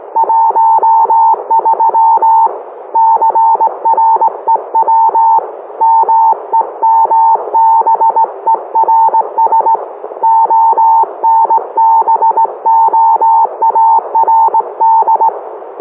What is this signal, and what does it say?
Morse code, heard as a keyed tone, about 18 wpm, 900 Hz. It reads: 13 CREW MEMBERS ONBOARD